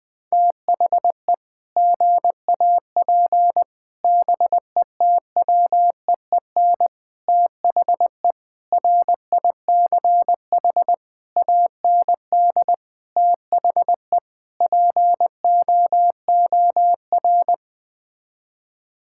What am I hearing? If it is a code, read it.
Morse code, heard as a keyed tone, about 20 wpm, 700 Hz. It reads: THE GAP BETWEEN THE RICH AND THE POOR
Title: Sets of 8 Words - 20wpm